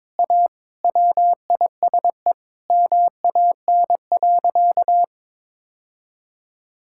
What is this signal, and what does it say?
Morse code, heard as a keyed tone, about 22 wpm, 700 Hz. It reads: A WISE MAN.
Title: Sentences from Top 1600 Words 22wpm